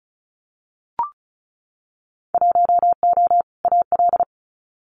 Morse code, heard as a keyed tone, about 34 wpm, 700 Hz. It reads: 1O AL
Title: Winter Field Day Exchanges 35wpm